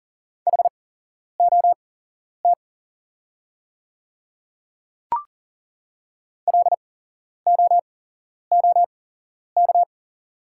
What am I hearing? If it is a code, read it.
Morse code, heard as a keyed tone, about 40 wpm, spaced as for 11 wpm, 690 Hz. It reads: HOT LOOK
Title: 8x Character Spacing - Top 100 Words - 40wpm